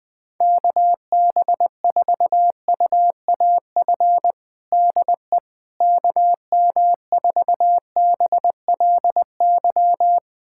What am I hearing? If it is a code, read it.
Morse code, heard as a keyed tone, about 21 wpm, 700 Hz. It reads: KB4UAF DE KM4BLY
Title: USA Callsign DE Callsign 20wpm